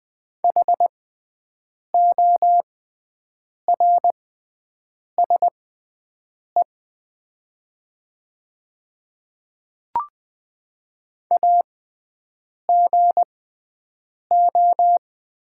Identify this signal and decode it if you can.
Morse code, heard as a keyed tone, about 20 wpm, spaced as for 7 wpm, 700 Hz. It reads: HORSE AGO
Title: 6x Character Spacing - Top 500 Words - 20wpm